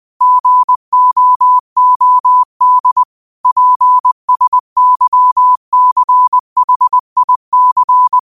Morse code, heard as a keyed tone, about 20 wpm, 1000 Hz. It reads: GOOD PSYCHIC